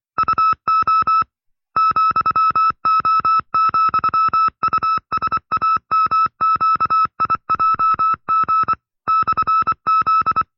Morse code, heard as a keyed tone, about 25 wpm, 1310 Hz. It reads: UO ,O,USAMQIJG /Z
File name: FreeFiles / CW-practice / 20WPM-side2